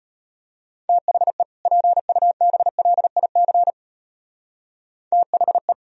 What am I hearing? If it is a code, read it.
Morse code, heard as a keyed tone, about 32 wpm, 700 Hz. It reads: THE PUBLIC THE